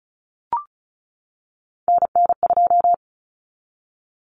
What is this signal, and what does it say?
Morse code, heard as a keyed tone, about 34 wpm, 700 Hz. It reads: NN2
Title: Common Cut Numbers - 3-Digit 35wpm